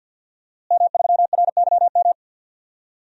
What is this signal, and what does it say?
Morse code, heard as a keyed tone, about 52 wpm, 700 Hz. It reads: M3RYM
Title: DXCC-100 Call Signs - 5 Characters - Repeated 2x - 50wpm